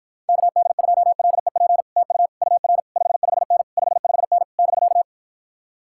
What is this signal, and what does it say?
Morse code, heard as a keyed tone, about 54 wpm, 700 Hz. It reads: KD2LEP TU UR 55N 55N <BK>